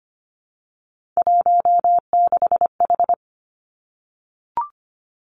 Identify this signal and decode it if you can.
Morse code, heard as a keyed tone, about 25 wpm, 700 Hz. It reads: 16H
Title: Three Challenging Letter-Number Combinations 25wpm